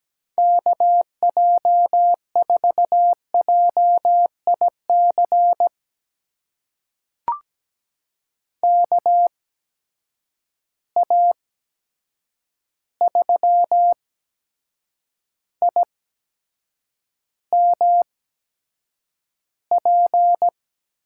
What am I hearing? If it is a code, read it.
Morse code, heard as a keyed tone, about 17 wpm, spaced as for 7 wpm, 700 Hz. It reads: KJ4JIC KA3IMP
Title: USA Call Signs - 6 Characters - 8x Character Spacing - 17wpm